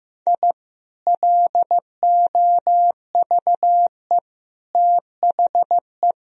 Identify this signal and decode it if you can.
Morse code, heard as a keyed tone, about 15 wpm, 700 Hz. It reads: I LOVE THE
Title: Sentences from Top 1000 Words - Review Part 4 - 15wpm